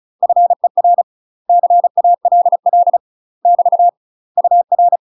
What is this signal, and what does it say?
Morse code, heard as a keyed tone, about 34 wpm, 700 Hz. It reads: FER CALL <BT> UR
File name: UK MASTER QSOs TRAINING / 35WPM / UK MASTER QSOs 0005